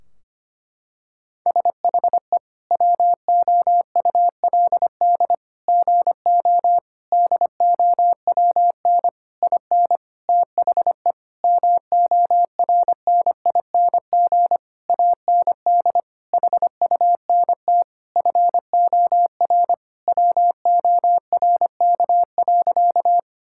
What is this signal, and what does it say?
Morse code, heard as a keyed tone, about 25 wpm, 700 Hz. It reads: SHE WOULD GO DOWN IN THE MORNING AND HUNT FOR WORK.
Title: Sentences from Top 1600 Words 25wpm